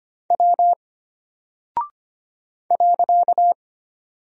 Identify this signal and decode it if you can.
Morse code, heard as a keyed tone, about 25 wpm, 700 Hz. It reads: W .
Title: Single Character (Letter:Number:Punctuation:Prosign) Alt 25wpm